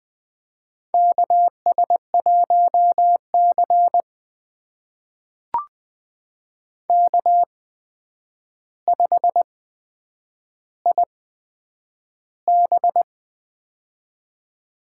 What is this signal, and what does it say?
Morse code, heard as a keyed tone, about 20 wpm, spaced as for 9 wpm, 700 Hz. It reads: KS1C K5IB
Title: USA Call Signs - 4 Characters - 8x Character Spacing - 20wpm